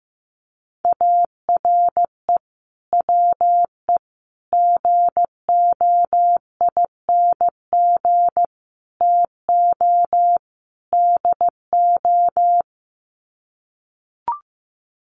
Morse code, heard as a keyed tone, about 15 wpm, 700 Hz. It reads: ARE WE GOING TO DO
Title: Sets of 5 Words 15wpm